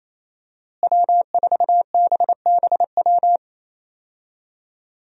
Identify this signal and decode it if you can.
Morse code, heard as a keyed tone, about 28 wpm, 700 Hz. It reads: W4BBW
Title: USA Call Signs - 5 Characters - Repeated 3x - 28wpm